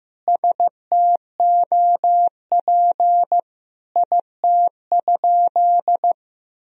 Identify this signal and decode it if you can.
Morse code, heard as a keyed tone, about 16 wpm, 700 Hz. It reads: STOP IT?
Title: Sentences from Top 300 Words and Abbreviations 15wpm